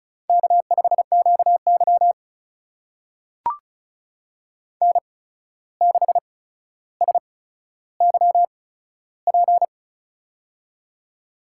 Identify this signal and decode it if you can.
Morse code, heard as a keyed tone, about 36 wpm, spaced as for 14 wpm, 700 Hz. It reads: K5QY N6SYP
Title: USA Call Signs - 8x Character Spacing - 35wpm